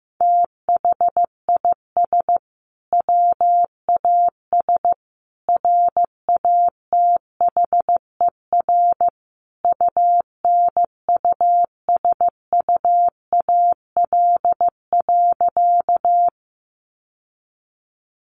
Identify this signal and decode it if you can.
Morse code, heard as a keyed tone, about 15 wpm, 700 Hz. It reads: THIS WAS RATHER UNUSUAL.
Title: Sentences from Top 1400 Words 15wpm